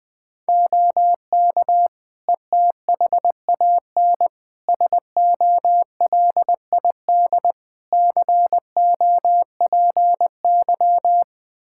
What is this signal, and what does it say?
Morse code, heard as a keyed tone, about 20 wpm, 700 Hz. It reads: OK ETHAN SOLID COPY
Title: Practice QSO - Exchange 3 - More Info - 20wpm